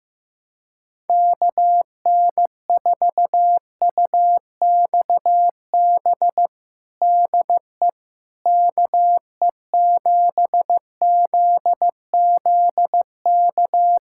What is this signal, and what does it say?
Morse code, heard as a keyed tone, about 15 wpm, 700 Hz. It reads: KN4UXB DE KE7ZZK
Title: USA Callsign DE Callsign - 6 Characters - 15wpm